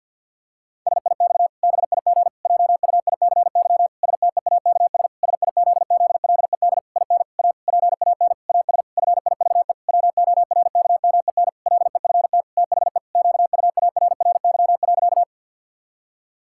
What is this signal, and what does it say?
Morse code, heard as a keyed tone, about 53 wpm, 700 Hz. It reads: SIX BIG JUICY STEAKS SIZZLED IN A PAN AS FIVE WORKMEN LEFT THE QUARRY.
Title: Pangram Challenge v2 - 50wpm